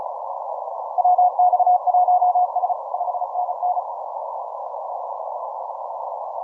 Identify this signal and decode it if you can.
Morse code, heard as a keyed tone, about 34 wpm, 720 Hz. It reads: WX1S 5NN
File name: simple-file-list / 6O6O 15MCW